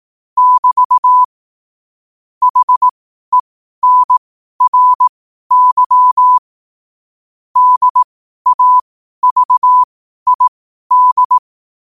Morse code, heard as a keyed tone, about 18 wpm, spaced as for 12 wpm, 1000 Hz. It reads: <BT> HENRY DAVID
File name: qotdfiles / 12wpm